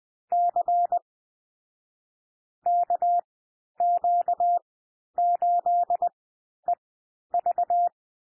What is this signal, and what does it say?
Morse code, heard as a keyed tone, about 20 wpm, spaced as for 11 wpm, 700 Hz. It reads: C KQ8EV
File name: cwmp3 / M1 SOUND MIXED 20 wpm